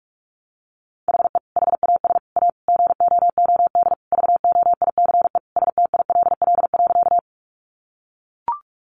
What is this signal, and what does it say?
Morse code, heard as a keyed tone, about 44 wpm, 700 Hz. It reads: HE HAS A GOOD VOICE STILL.
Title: Sentences from Top 500 Words 45wpm